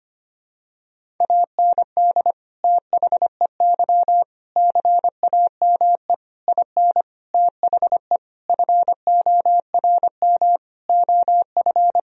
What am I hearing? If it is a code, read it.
Morse code, heard as a keyed tone, about 25 wpm, 700 Hz. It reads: AND THEY CAME IN THE FORM OF
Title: Sentences from Top 1000 Words - Review Part 1 - 25wpm